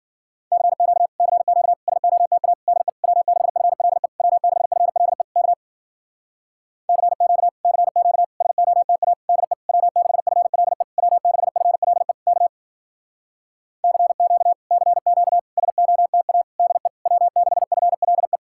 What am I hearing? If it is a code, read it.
Morse code, heard as a keyed tone, about 54 wpm, 700 Hz. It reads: CQ CQ SOTA DE W6FLE W6FLE K CQ CQ SOTA DE W6FLE W6FLE K CQ CQ SOTA DE W6FLE